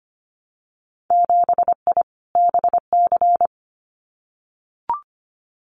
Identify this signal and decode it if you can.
Morse code, heard as a keyed tone, about 25 wpm, 700 Hz. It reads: 7I BC
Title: Winter Field Day Exchanges Encore 25wpm